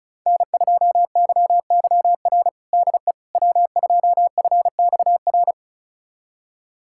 Morse code, heard as a keyed tone, about 36 wpm, 700 Hz. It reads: N2YYR DE W2FXR
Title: USA Callsign DE Callsign - 5 Characters - 35wpm